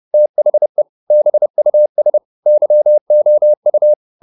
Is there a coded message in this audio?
Yes: Morse code from 0.1 to 3.9 s, heard as a keyed tone, about 30 wpm, 600 Hz. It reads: THE BUS YOU